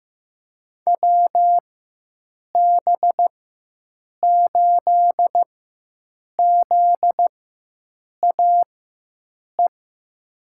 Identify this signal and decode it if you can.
Morse code, heard as a keyed tone, about 15 wpm, spaced as for 7 wpm, 700 Hz. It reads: WB8ZAE
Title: USA Call Signs - 6 Characters - 4x Character Spacing - 15wpm